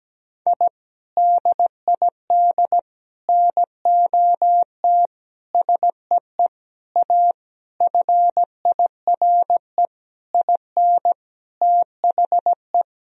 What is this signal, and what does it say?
Morse code, heard as a keyed tone, about 17 wpm, 700 Hz. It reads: I DID NOT SEE A FIRE IN THE
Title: Sentences from Top 1000 Words - Review Part 2 - 17wpm